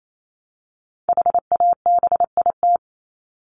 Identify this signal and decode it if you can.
Morse code, heard as a keyed tone, about 28 wpm, 700 Hz. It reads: HABIT
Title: Top 1751-2000 Words 28wpm